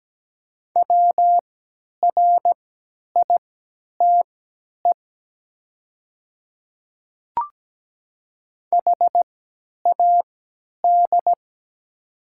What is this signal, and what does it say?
Morse code, heard as a keyed tone, about 17 wpm, spaced as for 10 wpm, 700 Hz. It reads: WRITE HAD
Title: 3x Character Spacing - Top 200 Words - 17wpm